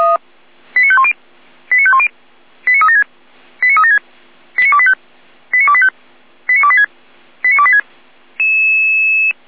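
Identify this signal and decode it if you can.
Five-tone selective call (ZVEI), heard as quick runs of five tones, tones of about 70 ms. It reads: 89510 89510 89376 89376 89285 89285 89286 89286
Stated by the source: ZVEI five-tone selective call 89280 received by SDR.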